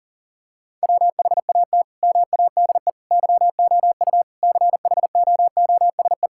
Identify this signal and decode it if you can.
Morse code, heard as a keyed tone, about 40 wpm, 700 Hz. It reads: WHAT MADE YOU CHOOSE
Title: Sentences from Top 1000 Words - Review Part 5 - 40wpm